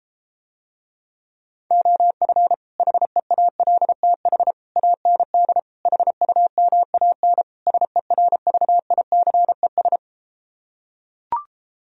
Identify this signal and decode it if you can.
Morse code, heard as a keyed tone, about 33 wpm, 700 Hz. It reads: OF HEALTH AND HUMAN SERVICES